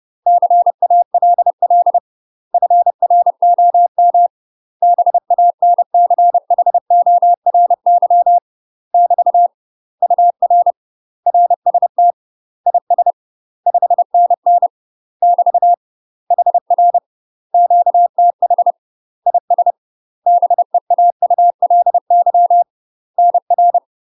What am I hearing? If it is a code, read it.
Morse code, heard as a keyed tone, about 30 wpm, 700 Hz. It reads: CALL FROM BANCHORY <BT> UR RST IS 5NN <BT> HR QTH IS BEAULY NR
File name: UK MASTER QSOs TRAINING / 30WPM / UK MASTER QSOs 0041